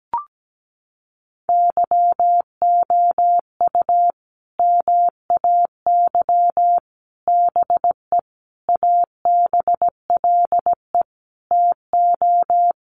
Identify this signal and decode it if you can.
Morse code, heard as a keyed tone, about 17 wpm, 700 Hz. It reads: YOU MAY BE ABLE TO
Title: Sets of 5 Words Encore 17wpm